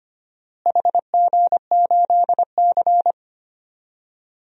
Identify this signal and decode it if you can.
Morse code, heard as a keyed tone, about 25 wpm, 700 Hz. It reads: HG8C
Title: DXCC-100 Call Signs - 4 Characters - Repeated 3x - 25wpm